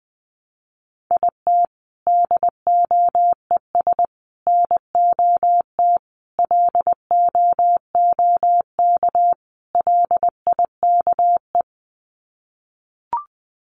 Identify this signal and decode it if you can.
Morse code, heard as a keyed tone, about 20 wpm, 700 Hz. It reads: IT DOES NOT LOOK LIKE
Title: Sets of 5 Words Encore2 20wpm